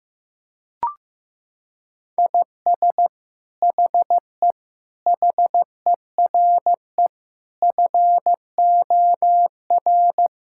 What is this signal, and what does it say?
Morse code, heard as a keyed tone, about 15 wpm, 700 Hz. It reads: IS HE HERE FOR
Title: Sentences from Top 1600 Words 15wpm